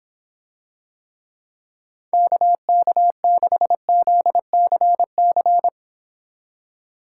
Morse code, heard as a keyed tone, about 26 wpm, 700 Hz. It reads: KK6ZCC